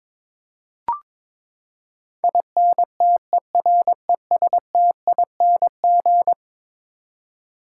Morse code, heard as a keyed tone, about 22 wpm, 700 Hz. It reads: INTERESTING
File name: practice / 22wpm / Eleven Letter Words - 22wpm